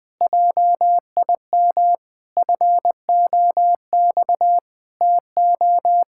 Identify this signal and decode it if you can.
Morse code, heard as a keyed tone, about 20 wpm, 700 Hz. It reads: JIM FOX TO
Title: Pangram Challenge v2 - 20wpm